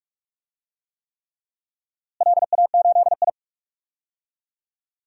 Morse code, heard as a keyed tone, about 46 wpm, 700 Hz. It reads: LA8I
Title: DXCC-100 Call Signs - 4 Characters - Repeated 3x - 45wpm